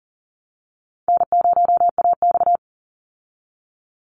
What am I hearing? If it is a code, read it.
Morse code, heard as a keyed tone, about 40 wpm, 700 Hz. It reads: N0AX
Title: Call Signs Repeated 3x - 40wpm